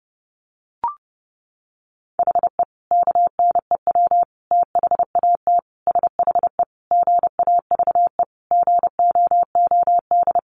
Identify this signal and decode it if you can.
Morse code, heard as a keyed tone, about 30 wpm, 700 Hz. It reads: HE KNEW THAT SHE GAVE GOOD
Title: Sentences from Top 1100 Words 30wpm